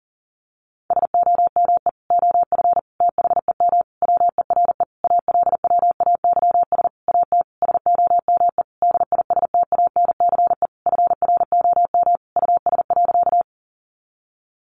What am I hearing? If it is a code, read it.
Morse code, heard as a keyed tone, about 37 wpm, 700 Hz. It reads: SOME OF THEM WERE ALWAYS AT SOME DISTANCE FROM US.
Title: Sentences from Top 1200 Words 40wpm